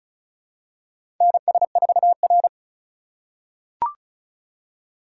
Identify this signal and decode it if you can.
Morse code, heard as a keyed tone, about 34 wpm, 700 Hz. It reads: NS4R
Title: USA Call Signs - 4 Characters - 4x Character Spacing - 35wpm